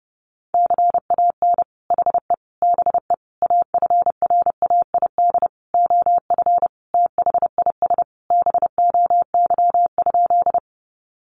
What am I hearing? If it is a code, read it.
Morse code, heard as a keyed tone, about 30 wpm, 700 Hz. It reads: CAN HE BE AFRAID OF THIS BOY?
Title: Sentences from Top 1500 Words 30wpm